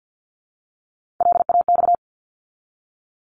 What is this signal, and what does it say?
Morse code, heard as a keyed tone, about 50 wpm, 700 Hz. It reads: LAX
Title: ARRL Sections 50wpm